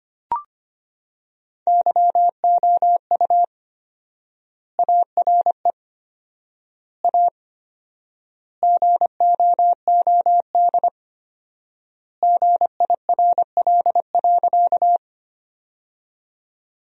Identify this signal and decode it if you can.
Morse code, heard as a keyed tone, about 25 wpm, 700 Hz. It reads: YOU ARE A GOOD GIRL.
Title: 4x Word-Spacing - Sentences from Top 400 Words - 25wpm